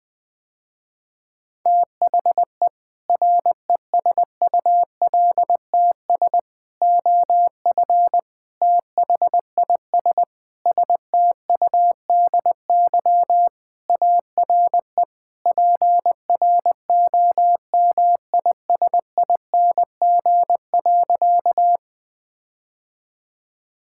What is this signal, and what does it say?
Morse code, heard as a keyed tone, about 20 wpm, 700 Hz. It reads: THE RESULTS OF THIS STUDY ARE PROMISING.
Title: Sentences with 5-Word Phrases - Spoken Beginning - 20wpm